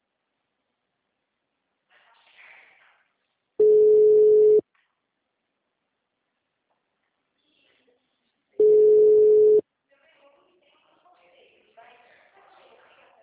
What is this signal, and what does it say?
Ringback tone (Europe, 425 Hz), with speech.